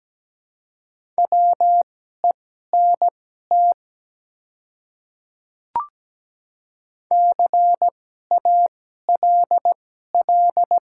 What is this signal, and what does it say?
Morse code, heard as a keyed tone, about 17 wpm, spaced as for 12 wpm, 700 Hz. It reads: WENT CALL
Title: 2x Character Spacing - Top 200 Words - 17wpm